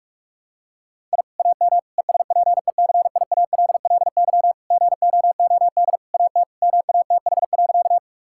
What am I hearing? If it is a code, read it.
Morse code, heard as a keyed tone, about 41 wpm, 700 Hz. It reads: I AM ESPECIALLY GOOD AT MATH.
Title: Sentences from Top 1000 Words - Review Part 6 - 45wpm